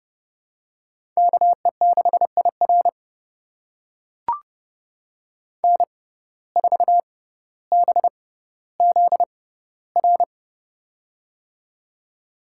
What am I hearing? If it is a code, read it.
Morse code, heard as a keyed tone, about 30 wpm, spaced as for 15 wpm, 700 Hz. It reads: KE6IR N4BZR
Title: USA Call Signs - 5 Characters - 6x Character Spacing - 30wpm